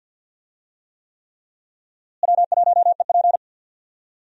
Morse code, heard as a keyed tone, about 51 wpm, 700 Hz. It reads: W1EP
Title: USA Call Signs - 4 Characters - Repeated 3x - 50wpm